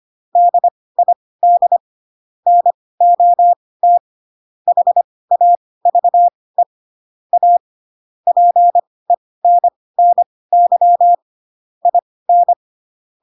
Morse code, heard as a keyed tone, about 25 wpm, spaced as for 18 wpm, 700 Hz. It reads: DID NOT HAVE A PENNY IN